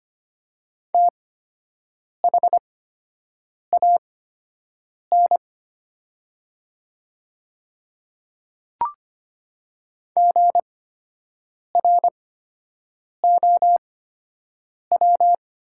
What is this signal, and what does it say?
Morse code, heard as a keyed tone, about 25 wpm, spaced as for 7 wpm, 700 Hz. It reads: THAN GROW
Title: 8x Character Spacing - Top 500 Words - 25wpm